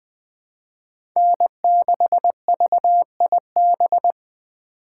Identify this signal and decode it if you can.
Morse code, heard as a keyed tone, about 20 wpm, 700 Hz. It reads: N6VIB
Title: USA Call Signs - 5 Characters - 6x Character Spacing - 20wpm